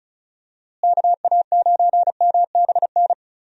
Morse code, heard as a keyed tone, about 36 wpm, 700 Hz. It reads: KA9MBN
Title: USA Call Signs - 6 Characters - Repeated 2x - 35wpm